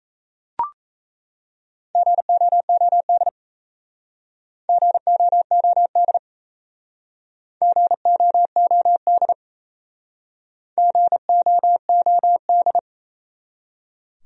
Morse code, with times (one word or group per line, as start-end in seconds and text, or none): 2.0-3.3 GOOD
4.7-6.2 GOOD
7.6-9.3 GOOD
10.8-12.8 GOOD